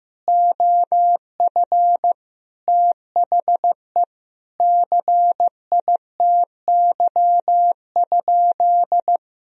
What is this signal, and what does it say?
Morse code, heard as a keyed tone, about 16 wpm, 700 Hz. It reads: OF THE CITY?
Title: Sentences from Top 1000 Words - Review Part 3 - 15wpm